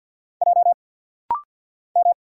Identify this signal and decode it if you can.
Morse code, heard as a keyed tone, about 50 wpm, 700 Hz. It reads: J M
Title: Single Letter-Number - Rapid-Fire 50wpm